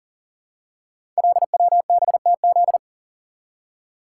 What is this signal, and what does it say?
Morse code, heard as a keyed tone, about 38 wpm, 690 Hz. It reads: LWBTZ
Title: Code Groups - 5 Letter:Numbers - 40wpm